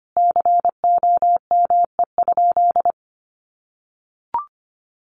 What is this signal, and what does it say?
Morse code, heard as a keyed tone, about 25 wpm, 700 Hz. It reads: COME?
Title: Sentences from Top 200 Words 25wpm